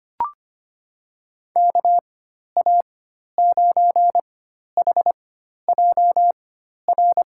Morse code, heard as a keyed tone, about 25 wpm, spaced as for 12 wpm, 700 Hz. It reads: KA9HJR
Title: USA Call Signs - 6 Characters - 4x Character Spacing - 25wpm